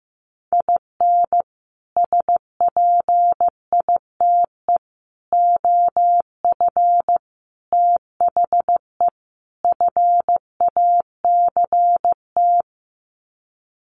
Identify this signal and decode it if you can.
Morse code, heard as a keyed tone, about 15 wpm, 700 Hz. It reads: IN SPITE OF THE FACT
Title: Sets of 5 Words Encore2 15wpm